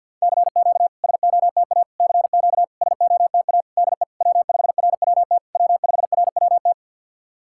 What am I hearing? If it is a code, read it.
Morse code, heard as a keyed tone, about 51 wpm, 660 Hz. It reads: CQ SOTA CQ SOTA DE W5RWT W5RWT
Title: SOTA QSO Practice 50wpm